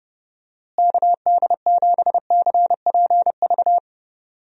Morse code, heard as a keyed tone, about 32 wpm, 700 Hz. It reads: KD7CPV
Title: USA Call Signs - 6 Characters - Repeated 2x - 30wpm